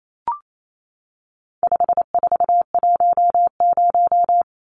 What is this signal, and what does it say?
Morse code, heard as a keyed tone, about 28 wpm, 700 Hz. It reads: 5410
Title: Numbers - 4-Digit 28wpm